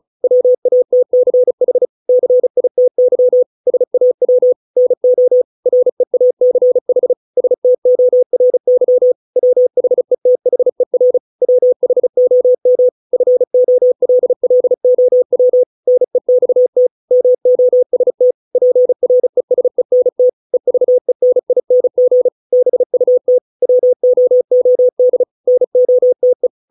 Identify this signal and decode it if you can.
Morse code, heard as a keyed tone, about 34 wpm, 500 Hz. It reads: WATCH CITY SAW NO REACH STORY WHETHER WHOM FOLLOW NEXT MOST PRESENT EVENING BUT WOOD NOTE